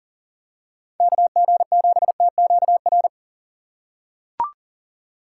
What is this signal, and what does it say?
Morse code, heard as a keyed tone, about 40 wpm, 700 Hz. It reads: KG7TQR
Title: USA Call Signs - 6 Characters - 3x Character Spacing - 40wpm